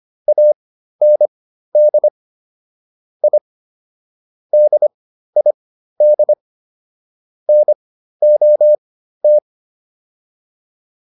Morse code, heard as a keyed tone, about 25 wpm, spaced as for 13 wpm, 600 Hz. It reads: AND I DID NOT
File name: WD302 13